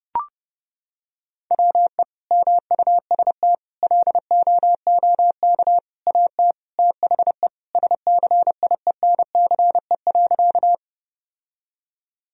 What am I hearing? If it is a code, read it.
Morse code, heard as a keyed tone, about 32 wpm, 700 Hz. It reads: WE MUST LOOK AT THE SCIENCE.
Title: Sentences from Top 1000 Words - Review Part 2 - 30wpm